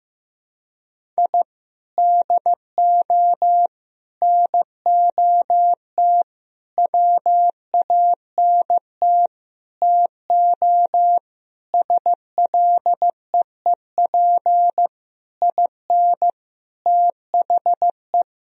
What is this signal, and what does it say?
Morse code, heard as a keyed tone, about 15 wpm, 700 Hz. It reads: I DO NOT WANT TO SLEEP IN THE
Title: Sentences from Top 1000 Words - Review Part 2 - 15wpm